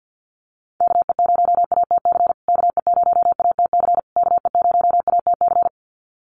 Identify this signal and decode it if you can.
Morse code, heard as a keyed tone, about 50 wpm, 700 Hz. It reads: KE0ATC KE0ATC KE0ATC
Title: USA Call Signs Repeated 3x 50wpm